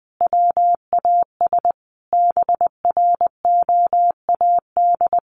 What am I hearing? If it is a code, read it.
Morse code, heard as a keyed tone, about 20 wpm, 700 Hz. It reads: WAS BROAD
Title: A Sound Of Thunder 20wpm